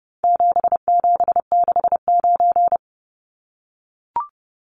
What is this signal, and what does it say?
Morse code, heard as a keyed tone, about 30 wpm, 700 Hz. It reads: 7769
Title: Numbers - 4-Digit 30wpm